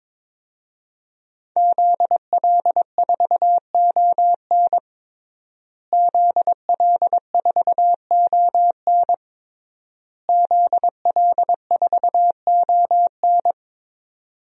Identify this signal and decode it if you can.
Morse code, heard as a keyed tone, about 22 wpm, 700 Hz. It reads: ZL4ON ZL4ON ZL4ON
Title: DXCC-100 Call Signs - 5 Characters - Repeated 3x - 22wpm